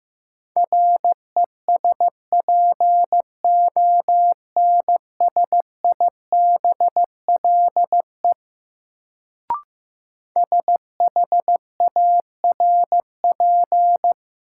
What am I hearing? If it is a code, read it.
Morse code, heard as a keyed tone, about 15 wpm, 700 Hz. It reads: RESPONSIBLE SHARP
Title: Sentences from Top 1600 Words - New Word Drill Alt - 15wpm